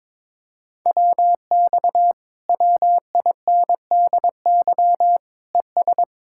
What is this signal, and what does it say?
Morse code, heard as a keyed tone, about 22 wpm, 700 Hz. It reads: WX WINDY ES
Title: Speed-Racing Common QSO Phrases - 22wpm